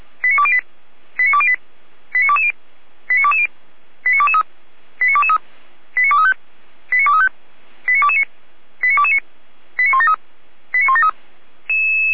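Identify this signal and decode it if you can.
Five-tone selective call (ZVEI), heard as quick runs of five tones, tones of about 70 ms. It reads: 89208 89208 89220 89220 89223 89223 89235 89235 89229 89229 89173 89173
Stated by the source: ZVEI five-tone selective call received on SDR.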